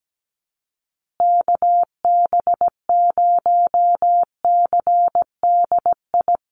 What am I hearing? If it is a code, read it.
Morse code, heard as a keyed tone, about 17 wpm, 700 Hz. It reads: KB0CDI
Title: USA Call Signs Repeated 2x 17wpm